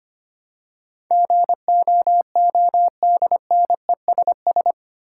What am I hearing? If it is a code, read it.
Morse code, heard as a keyed tone, about 25 wpm, 700 Hz. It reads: GOODNESS
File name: 8 Letter Words - First 3 Letters Given - 25wpm